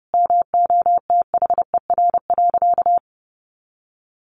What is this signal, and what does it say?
Morse code, heard as a keyed tone, about 30 wpm, 700 Hz. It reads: MOTHER.
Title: Sentences from Top 400 Words 30wpm